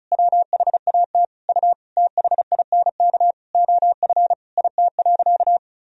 Morse code, heard as a keyed tone, about 34 wpm, 700 Hz. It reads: WHAT U THINK OF IT.
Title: Sentences from Top 200 Words and Abbreviations 35wpm